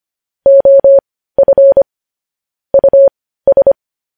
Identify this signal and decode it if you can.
Morse code, heard as a keyed tone, about 25 wpm, spaced as for 15 wpm, 550 Hz. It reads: OF US
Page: news-headlines-in-morse-code-at-15-wpm